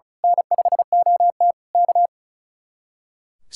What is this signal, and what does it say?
Morse code, heard as a keyed tone, about 33 wpm, 700 Hz. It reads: N5OT K
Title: Speed-Racing Common QSO Phrases - 35wpm